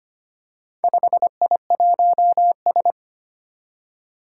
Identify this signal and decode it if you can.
Morse code, heard as a keyed tone, about 25 wpm, 700 Hz. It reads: 5I1S
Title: Four Challenging Letter-Number Combinations 25wpm